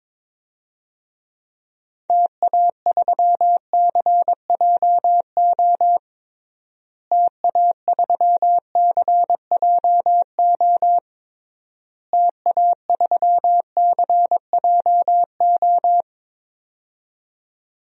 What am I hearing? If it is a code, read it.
Morse code, heard as a keyed tone, about 22 wpm, 700 Hz. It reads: TA3CJO TA3CJO TA3CJO
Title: DXCC-100 Call Signs - 6 Characters - Repeated 3x - 22wpm